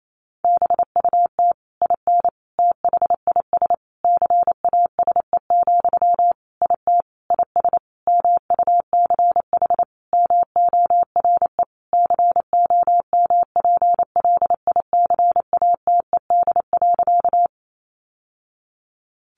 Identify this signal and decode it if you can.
Morse code, heard as a keyed tone, about 28 wpm, 700 Hz. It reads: BUT IN THIS CASE, IT IS MUCH MORE COMPLICATED.
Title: Sentences from Top 1200 Words 28wpm